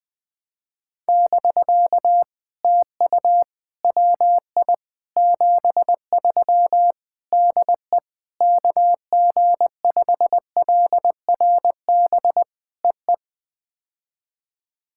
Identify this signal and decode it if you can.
Morse code, heard as a keyed tone, about 20 wpm, 700 Hz. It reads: <BK> TU WI 73 DE KG5LRB EE
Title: POTA QSO Practice - 20wpm